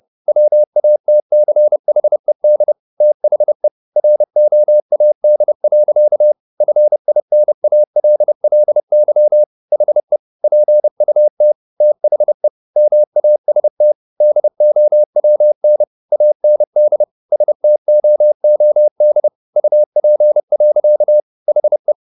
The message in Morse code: WATCHED THE ROAD. FINALLY HE PUT THE MAST DOWN AND STOOD UP. HE

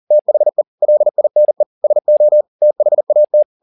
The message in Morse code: THE LINE SO THAT

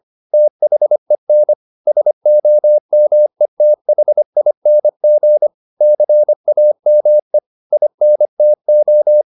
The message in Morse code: THEN SOMETHING CAME INTO

